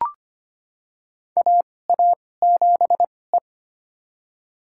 Morse code, heard as a keyed tone, about 25 wpm, spaced as for 18 wpm, 700 Hz. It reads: AA7E
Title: USA Call Signs - 4 Characters - 2x Character Spacing - 25wpm